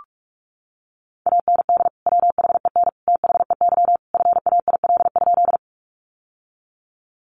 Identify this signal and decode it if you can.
Morse code, heard as a keyed tone, about 44 wpm, 700 Hz. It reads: AND WHEN THEY FAIL?
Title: Sentences from Top 1100 Words 45wpm